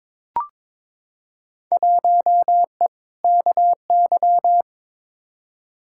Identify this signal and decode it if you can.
Morse code, heard as a keyed tone, about 22 wpm, 700 Hz. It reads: 1E KY
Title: Field Day Exchanges - 22wpm